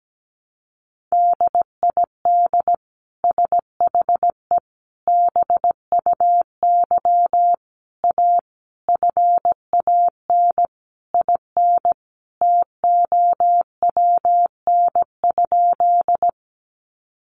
Morse code, heard as a keyed tone, about 17 wpm, 700 Hz. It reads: DID SHE BUY A FAN IN TOWN?
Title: Sentences from Top 1300 Words 17wpm